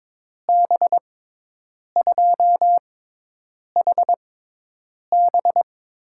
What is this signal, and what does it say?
Morse code, heard as a keyed tone, about 22 wpm, spaced as for 8 wpm, 700 Hz. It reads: B2HB
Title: USA Call Signs - 6 Characters - 6x Character Spacing - 22wpm